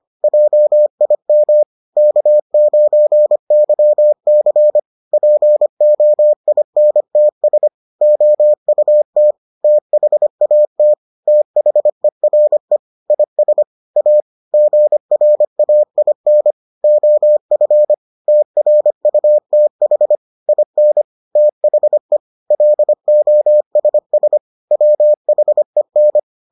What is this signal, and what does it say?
Morse code, heard as a keyed tone, about 25 wpm, 600 Hz. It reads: JIM K9YC POINTS OUT THAT THERE IS A GRAIN OF TRUTH IN THE LOSS WHEN